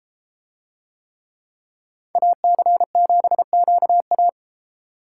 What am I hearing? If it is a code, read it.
Morse code, heard as a keyed tone, about 30 wpm, 700 Hz. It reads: AC7QA